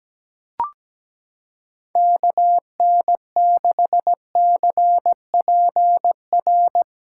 Morse code, heard as a keyed tone, about 17 wpm, 700 Hz. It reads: KN6CPR
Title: USA Callsign DE Callsign 17wpm